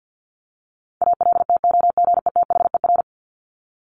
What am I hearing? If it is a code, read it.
Morse code, heard as a keyed tone, about 50 wpm, 700 Hz. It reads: ALTOGETHER
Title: Top 1501-1750 Words 50wpm